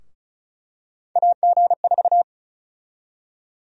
Morse code, heard as a keyed tone, about 34 wpm, 700 Hz. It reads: AG4
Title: USA Call Sign Prefixes - Repeated 2x - 35wpm